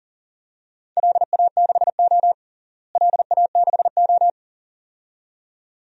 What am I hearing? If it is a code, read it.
Morse code, heard as a keyed tone, about 40 wpm, 700 Hz. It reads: LA6O LA6O
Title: DXCC-100 Call Signs - 4 Characters - Repeated 2x - 40wpm